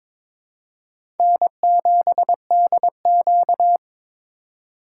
Morse code, heard as a keyed tone, about 22 wpm, 700 Hz. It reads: N7DQ
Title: USA Call Signs - 4 Characters - 4x Character Spacing - 22wpm